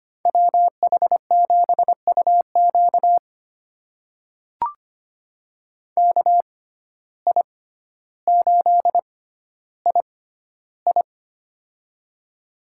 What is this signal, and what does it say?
Morse code, heard as a keyed tone, about 25 wpm, spaced as for 13 wpm, 700 Hz. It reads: WH7UQ KI8II
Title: USA Call Signs - 5 Characters - 6x Character Spacing - 25wpm